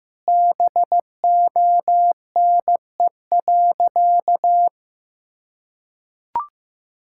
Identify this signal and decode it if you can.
Morse code, heard as a keyed tone, about 14 wpm, 700 Hz. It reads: BONE.
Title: Sentences from Top 1600 Words 15wpm